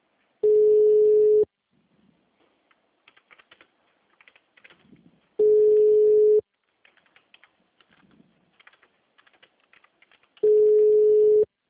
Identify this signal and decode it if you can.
Ringback tone (Europe, 425 Hz), with speech.
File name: out- -2005- - - .47375